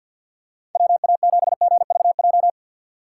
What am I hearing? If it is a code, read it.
Morse code, heard as a keyed tone, about 47 wpm, 700 Hz. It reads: WA7GVJ
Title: USA Call Signs - 6x Character Spacing - 50wpm